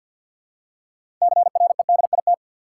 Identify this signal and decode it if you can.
Morse code, heard as a keyed tone, about 53 wpm, 700 Hz. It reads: CREDIT
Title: Six Letter Words - 50wpm